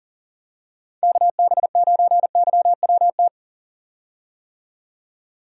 Morse code, heard as a keyed tone, about 40 wpm, 700 Hz. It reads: KB9YWT
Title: USA Call Signs - 6 Characters - 40wpm